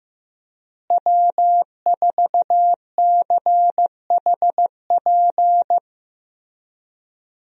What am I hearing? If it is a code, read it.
Morse code, heard as a keyed tone, about 15 wpm, 700 Hz. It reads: W4CHP
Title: USA Call Signs - 5 Characters - Repeated 2x - 15wpm